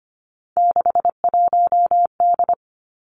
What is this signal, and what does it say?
Morse code, heard as a keyed tone, about 25 wpm, 700 Hz. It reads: 61D
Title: Three Challenging Letter-Number Combinations 25wpm